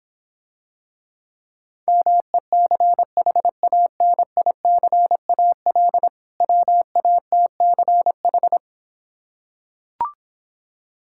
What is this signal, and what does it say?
Morse code, heard as a keyed tone, about 26 wpm, 700 Hz. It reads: MECHANICAL WATCH